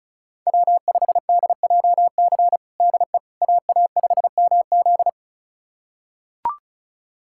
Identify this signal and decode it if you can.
Morse code, heard as a keyed tone, about 36 wpm, 700 Hz. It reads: W5DJC DE AA5MZ
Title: USA Callsign DE Callsign - 5 Characters - 35wpm